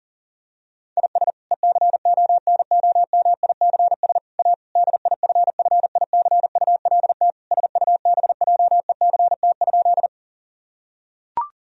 Morse code, heard as a keyed tone, about 40 wpm, 700 Hz. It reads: IS ECONOMICS A DIFFICULT SUBJECT?
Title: Sentences from Top 1000 Words - Review Part 3 - 40wpm